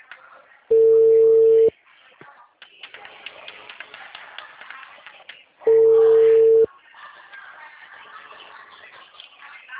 Ringback tone (China, 450 Hz).